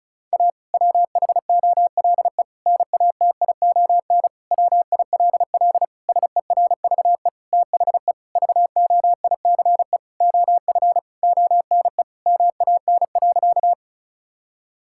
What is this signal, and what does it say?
Morse code, heard as a keyed tone, about 36 wpm, 700 Hz. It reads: A WHOLE NATION WILL SERVE THE VOICE OF ONE MAN.
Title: Sentences from Top 1000 Words - Review Part 6 - 35wpm